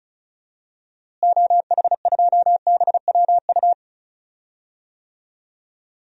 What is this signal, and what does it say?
Morse code, heard as a keyed tone, about 36 wpm, 700 Hz. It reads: OH2BWU
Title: DXCC-100 Call Signs - 6 Characters - Repeated 2x - 35wpm